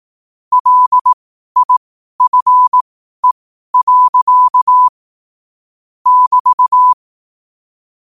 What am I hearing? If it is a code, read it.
Morse code, heard as a keyed tone, about 18 wpm, spaced as for 12 wpm, 1000 Hz. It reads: LIFE. <BT>
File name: qotdfiles / 12wpm